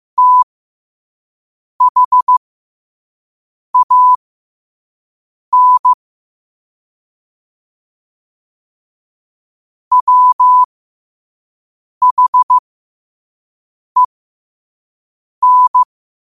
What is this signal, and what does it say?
Morse code, heard as a keyed tone, about 15 wpm, spaced as for 5 wpm, 1000 Hz. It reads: THAN WHEN
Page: Quotes for Wed, 13 Aug 2025 in Morse Code at 5 words per minute.